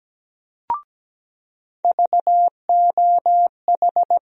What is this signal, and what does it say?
Morse code, heard as a keyed tone, about 17 wpm, 700 Hz. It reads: VOH
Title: USA Call Sign Postfixes - 3 Letters 17wpm